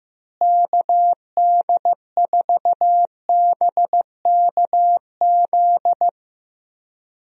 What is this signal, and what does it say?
Morse code, heard as a keyed tone, about 15 wpm, 700 Hz. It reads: KD4BKZ
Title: USA Call Signs - 6 Characters - Repeated 3x - 15wpm